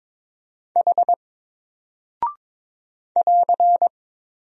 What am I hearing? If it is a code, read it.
Morse code, heard as a keyed tone, about 22 wpm, 700 Hz. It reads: H <AR>
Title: Single Character (Letter:Number:Punctuation:Prosign) 22wpm